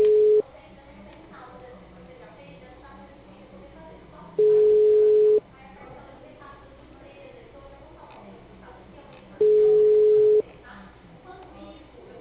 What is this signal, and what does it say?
Ringback tone (Europe, 425 Hz), with speech.
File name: out- -2019- - - .22